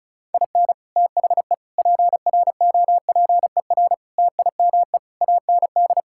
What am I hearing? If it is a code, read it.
Morse code, heard as a keyed tone, about 34 wpm, 700 Hz. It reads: IN THE PROPER TIME AND
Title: Sentences from Top 1000 Words - Review Part 5 - 35wpm